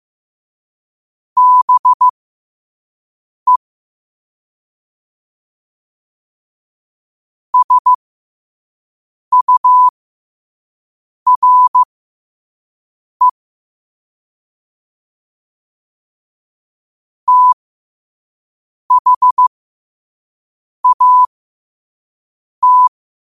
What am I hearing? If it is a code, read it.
Morse code, heard as a keyed tone, about 15 wpm, spaced as for 5 wpm, 1000 Hz. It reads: BE SURE THAT
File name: qotdfiles / 5wpm